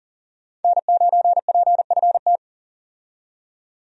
Morse code, heard as a keyed tone, about 40 wpm, 700 Hz. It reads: N9PFT
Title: USA Call Signs - 5 Characters - 40wpm